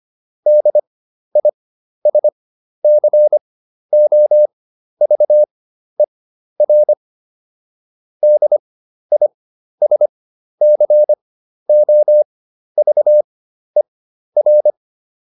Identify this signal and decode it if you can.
Morse code, heard as a keyed tone, about 25 wpm, spaced as for 12 wpm, 600 Hz. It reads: DISCOVER DISCOVER